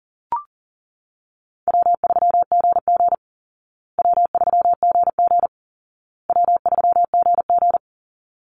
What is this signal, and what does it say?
Morse code, heard as a keyed tone, about 40 wpm, 700 Hz. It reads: W3GG W3GG W3GG
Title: Easy Call Signs Repeated 3x - 40wpm